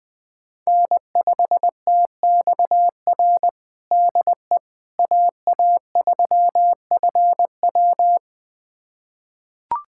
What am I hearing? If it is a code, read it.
Morse code, heard as a keyed tone, about 20 wpm, 700 Hz. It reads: N5TXR DE AA3FW
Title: USA Callsign DE Callsign - 5 Characters - 20wpm